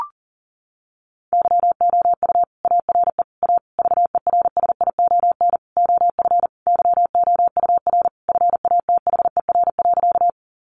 Morse code, heard as a keyed tone, about 40 wpm, 700 Hz. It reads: YOU ARE A VERSION OF YOUR FATHER.
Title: Sentences from Top 1400 Words 40wpm